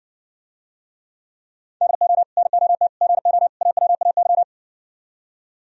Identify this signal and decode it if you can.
Morse code, heard as a keyed tone, about 60 wpm, 700 Hz. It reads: DO NOT GO AWAY